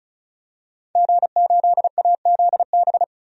Morse code, heard as a keyed tone, about 34 wpm, 700 Hz. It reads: G8AZB
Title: DXCC-100 Call Signs - 5 Characters - 35wpm